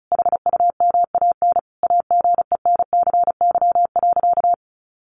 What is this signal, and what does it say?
Morse code, heard as a keyed tone, about 34 wpm, 700 Hz. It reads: HUMAN AGENCY.
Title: Sentences from Top 1300 Words 35wpm